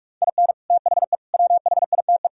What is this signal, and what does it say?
Morse code, heard as a keyed tone, about 44 wpm, 700 Hz. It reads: IN THE WHITE